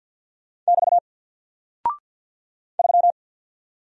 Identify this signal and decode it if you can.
Morse code, heard as a keyed tone, about 50 wpm, 700 Hz. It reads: <BT> 3
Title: Single Character (Letter:Number:Punctuation:Prosign) Alt 50wpm